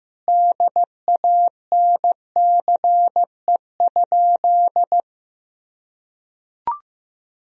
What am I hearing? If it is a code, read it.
Morse code, heard as a keyed tone, about 15 wpm, 700 Hz. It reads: DANCE?
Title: Sentences from Top 1000 Words - Review Part 4 - 15wpm